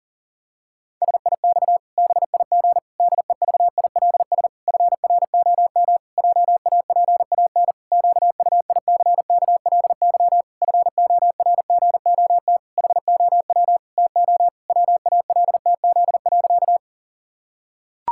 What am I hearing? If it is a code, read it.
Morse code, heard as a keyed tone, about 37 wpm, 700 Hz. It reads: SIX BIG DEVILS FROM JAPAN QUICKLY FORGOT HOW TO WALTZ.
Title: Pangram Challenge v2 - 40wpm